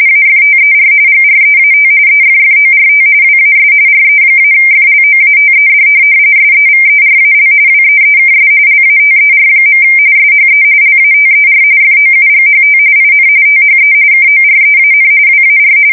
RTTY, about 45.45 baud. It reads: AFTER GREAT EFFORT WAS RE-FLOATED SEVERAL WEEKS LATER. SHE LATER FOUGHT IN THE KOREAN WAR
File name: fsk170 / USS-Missouri